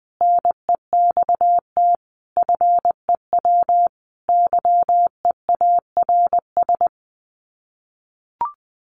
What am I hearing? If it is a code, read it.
Morse code, heard as a keyed tone, about 20 wpm, 700 Hz. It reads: NEXT FEW YEARS
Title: Sets of 5 Words Encore 20wpm